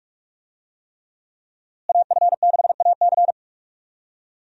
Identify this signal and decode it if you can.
Morse code, heard as a keyed tone, about 46 wpm, 700 Hz. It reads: AF6AC
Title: USA Call Signs - 5 Characters - Repeated 3x - 45wpm